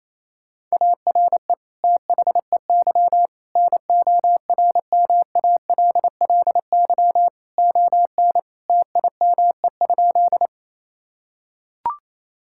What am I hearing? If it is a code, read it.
Morse code, heard as a keyed tone, about 28 wpm, 700 Hz. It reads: ARE THEY NORMALLY ON TIME?
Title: Sentences from Top 1000 Words - Review Part 6 - 28wpm